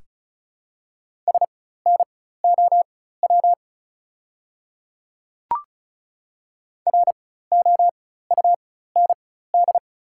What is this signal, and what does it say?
Morse code, heard as a keyed tone, about 36 wpm, spaced as for 16 wpm, 700 Hz. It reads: SNOW ROUND
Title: 4x Character Spacing - Top 500 Words - 35wpm